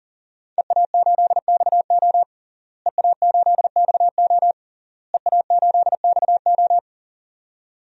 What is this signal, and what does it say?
Morse code, heard as a keyed tone, about 37 wpm, 700 Hz. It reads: EA8XO EA8XO EA8XO
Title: DXCC-100 Call Signs - 5 Characters - Repeated 3x - 40wpm